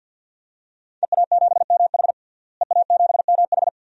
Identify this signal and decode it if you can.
Morse code, heard as a keyed tone, about 49 wpm, 700 Hz. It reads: EA7MH EA7MH
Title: DXCC-100 Call Signs - 5 Characters - Repeated 2x - 50wpm